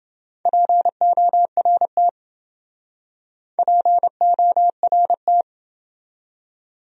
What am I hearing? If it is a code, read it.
Morse code, heard as a keyed tone, about 27 wpm, 700 Hz. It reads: PORT PORT